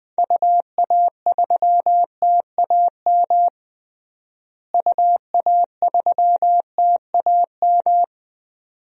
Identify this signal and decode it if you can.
Morse code, heard as a keyed tone, about 20 wpm, 700 Hz. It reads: UA3TAM UA3TAM
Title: DXCC-100 Call Signs - 6 Characters - Repeated 2x - 20wpm